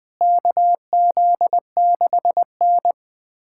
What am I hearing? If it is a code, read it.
Morse code, heard as a keyed tone, about 21 wpm, 700 Hz. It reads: KZ6N
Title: USA Callsign DE Callsign - 4 Characters - 20wpm